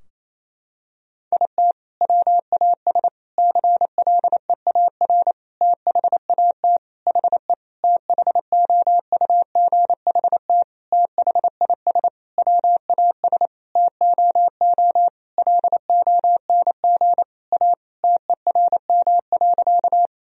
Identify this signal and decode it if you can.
Morse code, heard as a keyed tone, about 27 wpm, 700 Hz. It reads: IT WAS CLEAR THAT HE THOUGHT THIS WAS TOO LONG A TERM.
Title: Sentences from Top 1000 Words - Review Part 4 - 28wpm